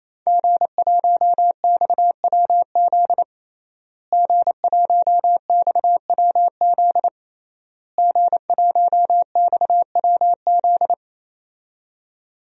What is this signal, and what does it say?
Morse code, heard as a keyed tone, about 28 wpm, 700 Hz. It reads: G1XWZ G1XWZ G1XWZ
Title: DXCC-100 Call Signs - 5 Characters - Repeated 3x - 28wpm